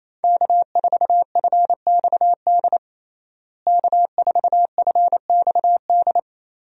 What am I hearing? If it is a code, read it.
Morse code, heard as a keyed tone, about 28 wpm, 700 Hz. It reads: K4FXD K4FXD
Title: USA Call Signs - 5 Characters - Repeated 3x - 28wpm